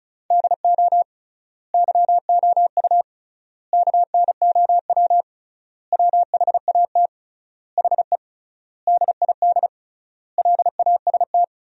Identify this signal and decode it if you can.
Morse code, heard as a keyed tone, about 36 wpm, 700 Hz. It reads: DO YOU KNOW WHAT HE DID LAST
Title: 3x Word-Spacing - Sentences from Top 300 Words - 35wpm